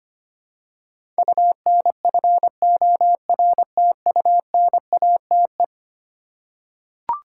Morse code, heard as a keyed tone, about 25 wpm, 700 Hz. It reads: UNFORTUNATE
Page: morse-code-ninja-practice-25wpm